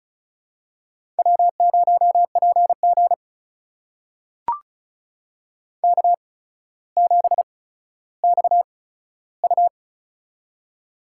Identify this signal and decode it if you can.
Morse code, heard as a keyed tone, about 36 wpm, spaced as for 15 wpm, 700 Hz. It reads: W0PG K7XU
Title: USA Call Signs - 4 Characters - 8x Character Spacing - 35wpm